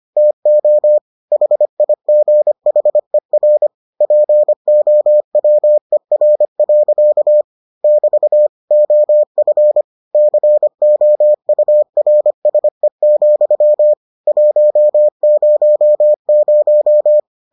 Morse code, heard as a keyed tone, about 25 wpm, 600 Hz. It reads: TO HIGHER POWER. <BT> OF COURSE, 100